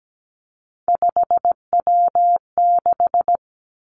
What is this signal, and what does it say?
Morse code, heard as a keyed tone, about 17 wpm, 700 Hz. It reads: 5W6
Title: Three Challenging Letter-Number Combinations 17wpm